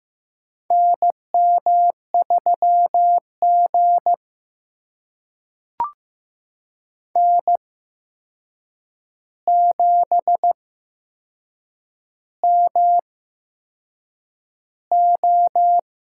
Morse code, heard as a keyed tone, about 15 wpm, spaced as for 6 wpm, 700 Hz. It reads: NM3G N7MO
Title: USA Call Signs - 4 Characters - 8x Character Spacing - 15wpm